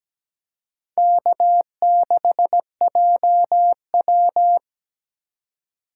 Morse code, heard as a keyed tone, about 20 wpm, 700 Hz. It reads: K6JW
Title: Speed Racing - USA Call Signs 17wpm